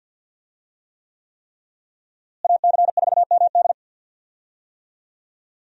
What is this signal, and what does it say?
Morse code, heard as a keyed tone, about 51 wpm, 700 Hz. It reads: AC4MD
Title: USA Call Signs - 5 Characters - Repeated 3x - 50wpm